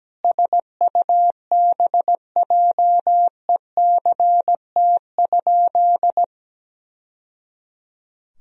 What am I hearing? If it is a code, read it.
Morse code, heard as a keyed tone, about 17 wpm, 700 Hz. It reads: SUBJECT?
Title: Sentences from Top 1000 Words - Review Part 3 - 17wpm